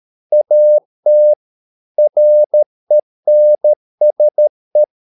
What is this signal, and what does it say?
Morse code, heard as a keyed tone, about 13 wpm, 600 Hz. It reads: AT RENSE
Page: Ferie 13 wpm